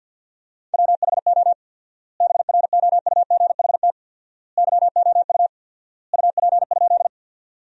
Morse code, heard as a keyed tone, about 52 wpm, 700 Hz. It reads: WHO BROUGHT YOU UP?
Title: 4x Word-Spacing - Sentences from Top 500 Words - 50wpm